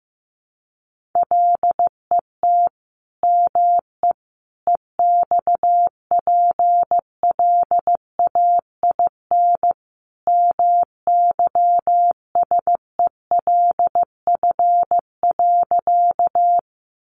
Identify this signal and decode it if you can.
Morse code, heard as a keyed tone, about 15 wpm, 700 Hz. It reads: LET ME EXPLAIN MYSELF.
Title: Sentences from Top 1300 Words 15wpm